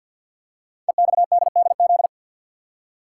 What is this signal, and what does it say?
Morse code, heard as a keyed tone, about 50 wpm, 690 Hz. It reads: EXDDZ
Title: Code Groups - 5 Letters - 50wpm